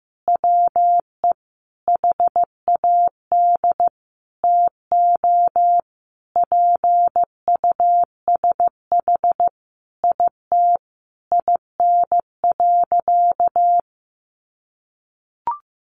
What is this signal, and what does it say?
Morse code, heard as a keyed tone, about 15 wpm, 700 Hz. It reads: WE HAD TO PUSH IT IN.
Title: Sentences from Top 1300 Words 15wpm